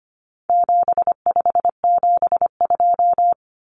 Morse code, heard as a keyed tone, about 25 wpm, 700 Hz. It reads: 7572
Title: Numbers - 4-Digit 25wpm